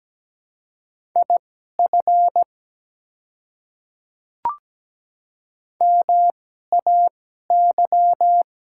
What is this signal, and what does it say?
Morse code, heard as a keyed tone, about 17 wpm, spaced as for 12 wpm, 690 Hz. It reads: IF MAY
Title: 2x Character Spacing - Top 100 Words - 17wpm